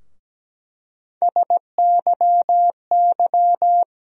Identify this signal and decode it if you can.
Morse code, heard as a keyed tone, about 17 wpm, 700 Hz. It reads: SYY
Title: USA Call Sign Postfixes - 3 Letters 17wpm